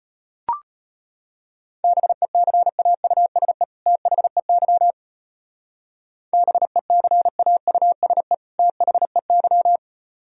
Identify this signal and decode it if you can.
Morse code, heard as a keyed tone, about 31 wpm, 700 Hz. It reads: BECAUSE THEY BECAUSE THEY
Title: Speed-Racing - Sets of 2 Words 25wpm